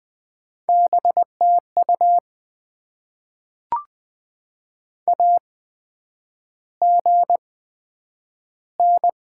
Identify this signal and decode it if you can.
Morse code, heard as a keyed tone, about 21 wpm, spaced as for 9 wpm, 700 Hz. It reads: BTU AGN
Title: Top 20 QSO Elements - 8x Character Spacing - 20wpm